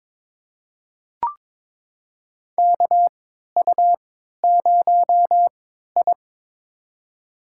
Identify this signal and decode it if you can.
Morse code, heard as a keyed tone, about 22 wpm, spaced as for 13 wpm, 700 Hz. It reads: KU0I
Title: USA Call Signs - 3x Character Spacing - 22wpm